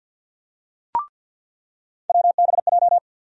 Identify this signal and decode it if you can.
Morse code, heard as a keyed tone, about 48 wpm, 700 Hz. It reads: WBJ
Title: USA Call Sign Postfixes - 3 Letters 50wpm